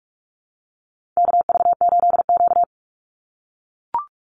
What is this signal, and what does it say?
Morse code, heard as a keyed tone, about 44 wpm, 700 Hz. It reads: KV8Q
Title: Call Signs - 45wpm